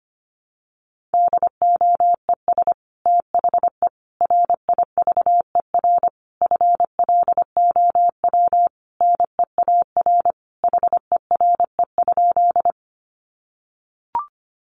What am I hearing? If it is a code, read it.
Morse code, heard as a keyed tone, about 25 wpm, 700 Hz. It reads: DOES THE RIVER FLOW NEAR HERE?
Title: Sentences from Top 1200 Words 25wpm